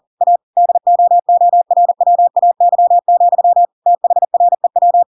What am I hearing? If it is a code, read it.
Morse code, heard as a keyed tone, about 40 wpm, 700 Hz. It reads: A DOORWAY, THREW